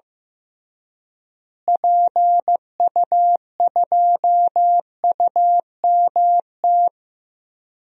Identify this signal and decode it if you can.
Morse code, heard as a keyed tone, about 15 wpm, 700 Hz. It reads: PU2UMT
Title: DXCC-100 Call Signs - 6 Characters - Repeated 3x - 15wpm